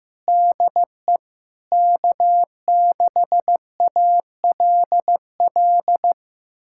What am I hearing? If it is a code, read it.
Morse code, heard as a keyed tone, about 15 wpm, 700 Hz. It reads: DE K6ALL
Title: USA Callsign DE Callsign 15wpm